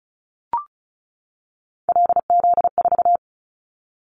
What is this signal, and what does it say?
Morse code, heard as a keyed tone, about 36 wpm, 700 Hz. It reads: LZ4
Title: Callsign Prefixes for Top 50 DXCC - 35wpm